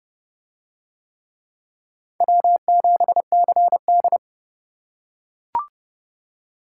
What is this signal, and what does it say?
Morse code, heard as a keyed tone, about 29 wpm, 700 Hz. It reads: W7CD